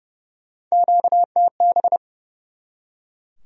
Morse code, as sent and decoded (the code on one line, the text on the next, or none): --.- - -...
QTB